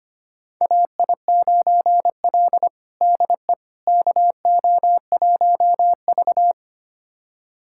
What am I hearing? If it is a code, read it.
Morse code, heard as a keyed tone, about 25 wpm, 700 Hz. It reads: AI9L DE KO1V
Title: USA Callsign DE Callsign - 4 Characters - 25wpm